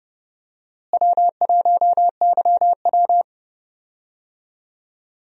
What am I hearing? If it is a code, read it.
Morse code, heard as a keyed tone, about 30 wpm, 700 Hz. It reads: W1YW
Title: USA Call Signs - 4 Characters - Repeated 2x - 30wpm